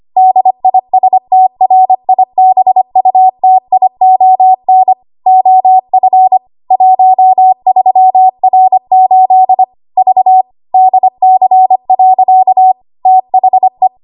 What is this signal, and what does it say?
Morse code, heard as a keyed tone, about 25 wpm, 750 Hz. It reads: DISTRIBUTION OF 13R8 V DC. THE
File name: Morse / 25WPM